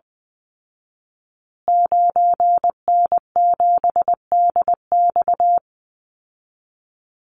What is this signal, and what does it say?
Morse code, heard as a keyed tone, about 20 wpm, 700 Hz. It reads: 9N7DX
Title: Call Signs Repeated 2x - 20wpm